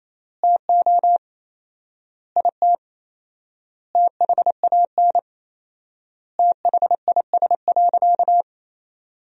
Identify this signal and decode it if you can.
Morse code, heard as a keyed tone, about 28 wpm, 700 Hz. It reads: TO IT THAN THIS.
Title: 4x Word-Spacing - Sentences from Top 100 Words - 28wpm